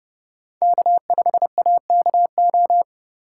Morse code, heard as a keyed tone, about 30 wpm, 700 Hz. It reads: K5AKO
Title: USA Call Signs - 5 Characters - 2x Character Spacing - 30wpm